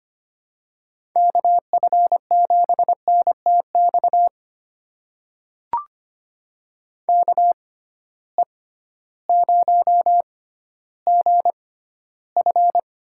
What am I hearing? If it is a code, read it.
Morse code, heard as a keyed tone, about 25 wpm, spaced as for 14 wpm, 700 Hz. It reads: KF7NTX KE0GF
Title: USA Call Signs - 6 Characters - 6x Character Spacing - 25wpm